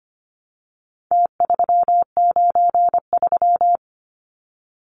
Morse code, heard as a keyed tone, about 25 wpm, 700 Hz. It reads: T393
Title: Numbers - Contest Style - 4-Digit 25wpm